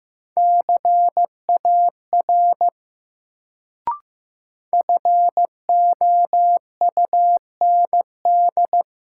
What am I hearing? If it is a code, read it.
Morse code, heard as a keyed tone, about 15 wpm, 700 Hz. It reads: CAR FOUND
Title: Sets of 5 Words Encore - New Word Drill - 15wpm